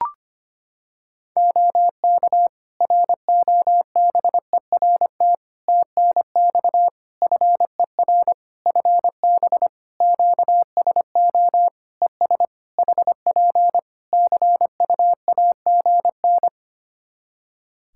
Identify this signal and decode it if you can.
Morse code, heard as a keyed tone, about 25 wpm, 700 Hz. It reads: OK ROBERT TNX FER FB QSO ES HP CUAGN
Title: K1USN SST QSO Practice 25wpm